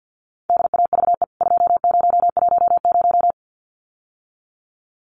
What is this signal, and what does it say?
Morse code, heard as a keyed tone, about 48 wpm, 700 Hz. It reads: DAVE 2010
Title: CWT Exchange 50wpm